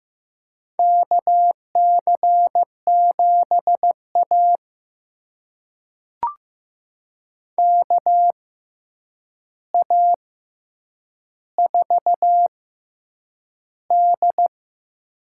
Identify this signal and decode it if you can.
Morse code, heard as a keyed tone, about 15 wpm, spaced as for 8 wpm, 700 Hz. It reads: KC7A KA4D
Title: USA Call Signs - 4 Characters - 6x Character Spacing - 15wpm